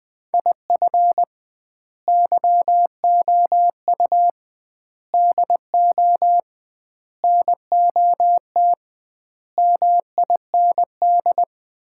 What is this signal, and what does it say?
Morse code, heard as a keyed tone, about 20 wpm, 700 Hz. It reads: IF YOU DO NOT MIND
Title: 2x Word-Spacing - Sets of 5 Words - 20wpm